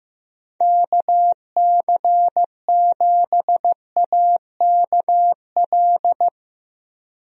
Morse code, heard as a keyed tone, about 15 wpm, 700 Hz. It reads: KC7AKL
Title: USA Call Signs - 6 Characters - Repeated 2x - 15wpm